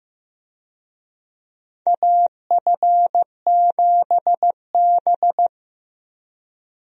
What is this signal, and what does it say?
Morse code, heard as a keyed tone, about 15 wpm, 700 Hz. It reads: AF7B
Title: USA Call Signs - 4 Characters - 15wpm